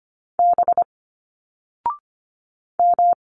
Morse code, heard as a keyed tone, about 26 wpm, 700 Hz. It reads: B M
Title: Single Letters 25wpm